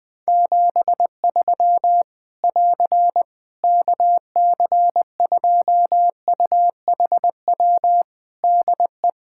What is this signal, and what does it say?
Morse code, heard as a keyed tone, about 20 wpm, 700 Hz. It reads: 73 <AR> KC2UHW DE
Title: Practice QSO - Exchange 4 - Ending - 20wpm